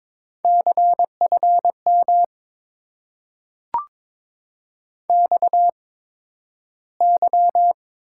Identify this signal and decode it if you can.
Morse code, heard as a keyed tone, about 22 wpm, spaced as for 12 wpm, 700 Hz. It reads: CFM XY
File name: Top 90 QSO Elements - 8x Character Spacing - 22wpm